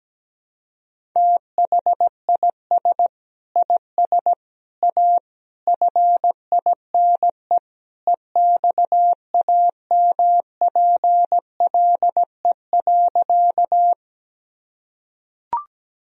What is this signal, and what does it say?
Morse code, heard as a keyed tone, about 17 wpm, 700 Hz. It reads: THIS IS A FINE EXAMPLE.
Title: Sentences from Top 1000 Words - Review Part 1 - 17wpm